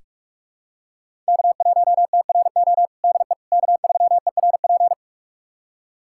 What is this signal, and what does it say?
Morse code, heard as a keyed tone, about 41 wpm, 700 Hz. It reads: K1TRO DE K3ERP
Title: USA Callsign DE Callsign 45wpm